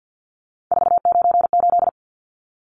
Morse code, heard as a keyed tone, about 50 wpm, 700 Hz. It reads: 498
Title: Numbers - 3-Digit 50wpm